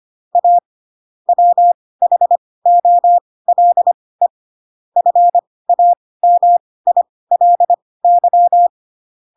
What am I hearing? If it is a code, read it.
Morse code, heard as a keyed tone, about 25 wpm, spaced as for 18 wpm, 700 Hz. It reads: A WHOLE FAMILY
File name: Pinocchio ch3 12 25 ik1hge